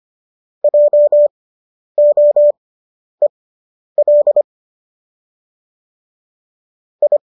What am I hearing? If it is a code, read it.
Morse code, heard as a keyed tone, about 25 wpm, spaced as for 10 wpm, 590 Hz. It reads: JOEL I